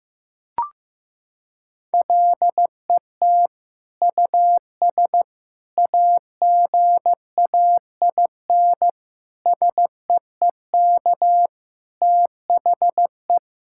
Morse code, heard as a keyed tone, about 15 wpm, 700 Hz. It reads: LET US AGAIN SEEK THE
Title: Sentences from Top 1600 Words 15wpm